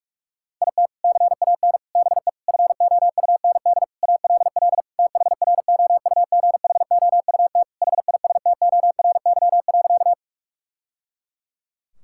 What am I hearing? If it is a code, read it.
Morse code, heard as a keyed tone, about 42 wpm, 700 Hz. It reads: IT CAN BE FOUND ALL THROUGHOUT HISTORY.
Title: Sentences from Top 1000 Words - Review Part 5 - 45wpm